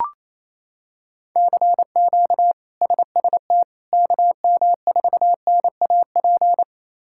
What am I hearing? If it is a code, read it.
Morse code, heard as a keyed tone, about 27 wpm, 700 Hz. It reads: CQ SST KM4NAP
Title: K1USN SST QSO Practice - 28wpm